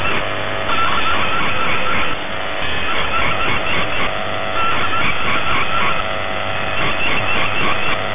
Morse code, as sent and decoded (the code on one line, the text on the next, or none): none